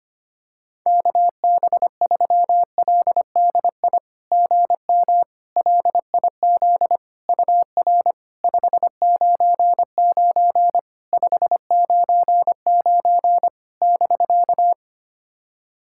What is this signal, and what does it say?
Morse code, heard as a keyed tone, about 25 wpm, 700 Hz. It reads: KB3LDI GM LIZ UR 599 599 <BK>
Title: SOTA QSO Practice 25wpm